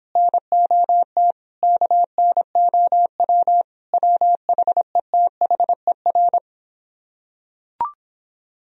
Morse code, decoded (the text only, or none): NOT KNOW WHETHER